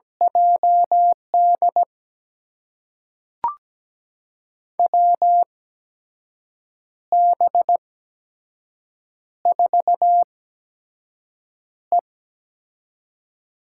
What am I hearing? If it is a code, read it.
Morse code, heard as a keyed tone, about 17 wpm, spaced as for 6 wpm, 700 Hz. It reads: JD WB4E
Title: USA Call Signs - 8x Character Spacing - 17wpm